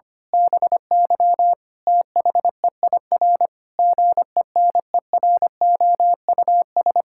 Morse code, heard as a keyed tone, about 25 wpm, 700 Hz. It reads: BY THEIR GENEROUS
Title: Pangram Challenge v2 - 25wpm